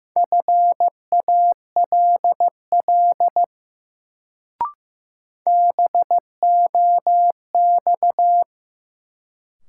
Morse code, heard as a keyed tone, about 15 wpm, 700 Hz. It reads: FALL BOX
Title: Sentences from Top 500 Words - New Word Drill - 15wpm